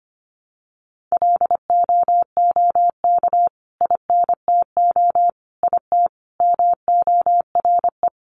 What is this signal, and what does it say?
Morse code, heard as a keyed tone, about 25 wpm, 700 Hz. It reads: LOOK INTO IT MORE
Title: Sentences from Top 1300 Words 25wpm